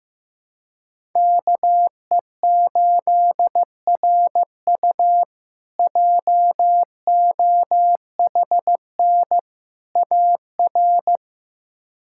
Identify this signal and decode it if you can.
Morse code, heard as a keyed tone, about 15 wpm, 700 Hz. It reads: KE8RU JOHN AR
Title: K1USN SST QSO Practice - 15wpm